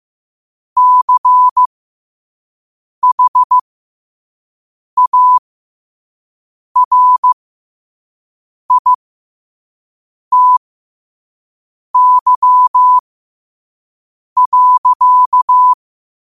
Morse code, heard as a keyed tone, about 15 wpm, spaced as for 5 wpm, 1000 Hz. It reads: CHARITY.
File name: qotdfiles / 5wpm